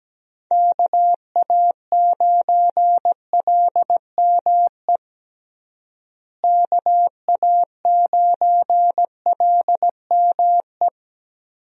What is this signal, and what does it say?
Morse code, heard as a keyed tone, about 17 wpm, 700 Hz. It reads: KA9LME KA9LME
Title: USA Call Signs - 6 Characters - Repeated 3x - 17wpm